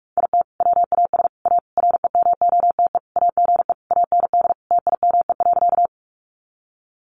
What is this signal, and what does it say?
Morse code, heard as a keyed tone, about 44 wpm, 700 Hz. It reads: IT WAS A REMOTE AGE AND TIME.
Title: Sentences from Top 1300 Words 45wpm